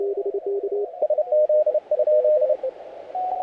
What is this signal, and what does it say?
Morse code, heard as a keyed tone, about 31 wpm, 610 Hz. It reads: SZ?
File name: audio mp3 files / A71BX 40m CW 2008-4-12 2-0-53